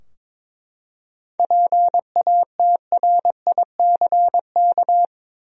Morse code, heard as a keyed tone, about 22 wpm, 700 Hz. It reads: PATRICK
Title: USA Common Names - 22wpm